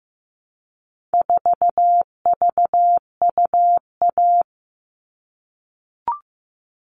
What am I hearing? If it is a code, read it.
Morse code, heard as a keyed tone, about 15 wpm, 700 Hz. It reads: 4VUA
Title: All Cut Numbers - 4-Digit 15wpm